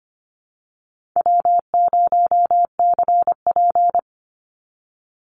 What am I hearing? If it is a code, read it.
Morse code, heard as a keyed tone, about 25 wpm, 700 Hz. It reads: W0CP
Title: Call Signs - 25wpm